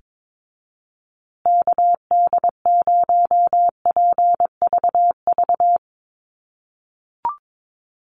Morse code, heard as a keyed tone, about 22 wpm, 700 Hz. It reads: KD0PVV
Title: USA Call Signs Repeated 3x 22wpm